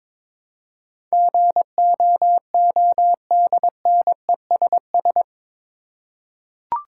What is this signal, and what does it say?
Morse code, heard as a keyed tone, about 21 wpm, 700 Hz. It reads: GOODNESS
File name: practice / 22wpm / Eight Letter Words - 22wpm